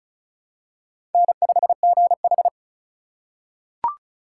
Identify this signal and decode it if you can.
Morse code, heard as a keyed tone, about 36 wpm, 700 Hz. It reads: N5GH
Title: USA Call Signs - 4x Character Spacing - 35wpm